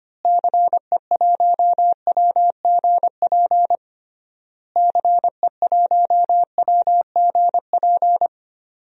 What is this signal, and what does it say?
Morse code, heard as a keyed tone, about 25 wpm, 690 Hz. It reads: CE1WGP CE1WGP
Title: DXCC-100 Call Signs - 6 Characters - Repeated 3x - 25wpm